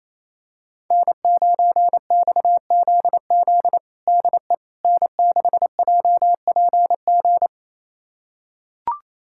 Morse code, heard as a keyed tone, about 28 wpm, 700 Hz. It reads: N9XZZ DE N6JPG
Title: USA Callsign DE Callsign - 5 Characters - 28wpm